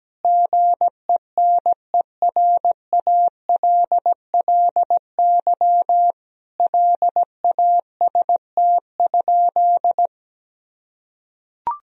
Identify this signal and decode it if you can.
Morse code, heard as a keyed tone, about 17 wpm, 700 Hz. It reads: GENERALLY LAST?
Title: Sentences from Top 1000 Words - Review Part 4 - 17wpm